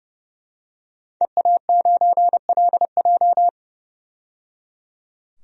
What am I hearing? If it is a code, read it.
Morse code, heard as a keyed tone, about 30 wpm, 700 Hz. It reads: EA9LJ
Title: DXCC-100 Call Signs - 5 Characters - 30wpm